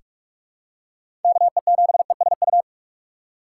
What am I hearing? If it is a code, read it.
Morse code, heard as a keyed tone, about 42 wpm, 700 Hz. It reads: K E 7 E S U
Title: USA Call Signs - 6 Characters - 6x Character Spacing - 45wpm